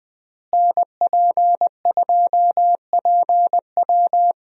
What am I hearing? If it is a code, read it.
Morse code, heard as a keyed tone, about 20 wpm, 700 Hz. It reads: NP2PW
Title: USA Call Signs - 5 Characters - Repeated 3x - 20wpm